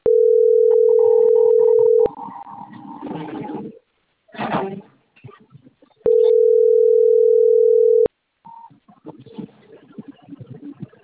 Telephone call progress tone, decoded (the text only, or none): ringback tone